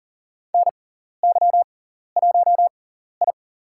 Morse code, heard as a keyed tone, about 38 wpm, spaced as for 14 wpm, 700 Hz. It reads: NY1I
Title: USA Call Signs - 4 Characters - 6x Character Spacing - 40wpm